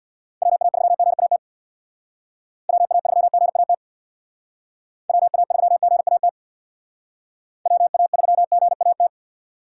Morse code, heard as a keyed tone, about 65 wpm, 700 Hz. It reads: WA3GAT WA3GAT WA3GAT WA3GAT